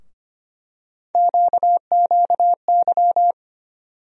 Morse code, heard as a keyed tone, about 25 wpm, 700 Hz. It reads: QQY
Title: USA Call Sign Postfixes - 3 Letters 25wpm